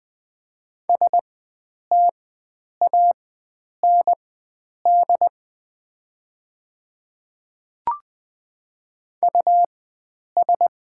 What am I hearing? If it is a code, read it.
Morse code, heard as a keyed tone, about 21 wpm, spaced as for 9 wpm, 700 Hz. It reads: STAND US
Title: 4x Character Spacing - Top 500 Words - 20wpm